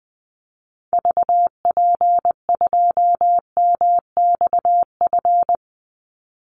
Morse code, heard as a keyed tone, about 20 wpm, 700 Hz. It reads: VP2MXF
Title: Call Signs Repeated 3x - 20wpm